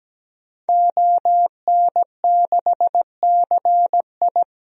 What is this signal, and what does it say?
Morse code, heard as a keyed tone, about 17 wpm, 700 Hz. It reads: ON6CI
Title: DXCC-100 Call Signs - 5 Characters - 17wpm